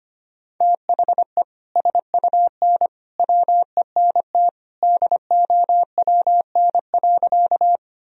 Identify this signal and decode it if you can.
Morse code, heard as a keyed tone, about 25 wpm, 700 Hz. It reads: THE SUN WENT DOWN.
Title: Sentences from Top 1000 Words - Review Part 1 - 25wpm